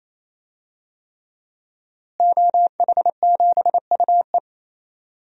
Morse code, heard as a keyed tone, about 28 wpm, 700 Hz. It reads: OH7UE
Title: DXCC-100 Call Signs - 5 Characters - Repeated 3x - 28wpm